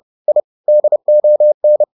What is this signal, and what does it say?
Morse code, heard as a keyed tone, about 30 wpm, 600 Hz. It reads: I DON
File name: the old man and the sea / the old man and the sea 1 30wpm